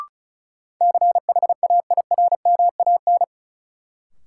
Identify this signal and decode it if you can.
Morse code, heard as a keyed tone, about 36 wpm, 700 Hz. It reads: CHAIRMAN
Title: Sets of 3 Words - New Word Drill - 35wpm